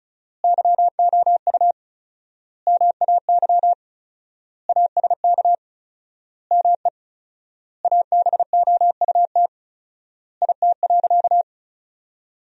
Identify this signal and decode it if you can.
Morse code, heard as a keyed tone, about 36 wpm, 700 Hz. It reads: YOU MAY ASK ME ABOUT IT.
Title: 4x Word-Spacing - Sentences from Top 200 Words - 35wpm